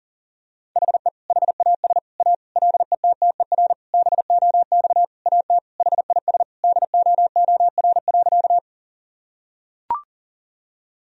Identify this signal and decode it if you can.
Morse code, heard as a keyed tone, about 40 wpm, 700 Hz. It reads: HE HAS A LETTER BOX AT HIS DOOR.
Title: Sentences from Top 1000 Words - Review Part 2 - 40wpm